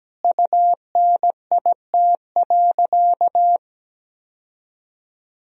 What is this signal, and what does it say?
Morse code, heard as a keyed tone, about 17 wpm, 700 Hz. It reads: UNIT.
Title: Sentences from Top 1000 Words - Review Part 2 - 17wpm